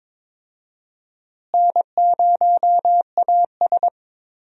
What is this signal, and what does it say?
Morse code, heard as a keyed tone, about 22 wpm, 700 Hz. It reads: N0AS
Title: USA Call Signs - 4 Characters - Repeated 2x - 22wpm